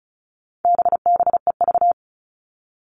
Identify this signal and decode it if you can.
Morse code, heard as a keyed tone, about 34 wpm, 700 Hz. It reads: BBEV
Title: All Cut Numbers - 4-Digit 35wpm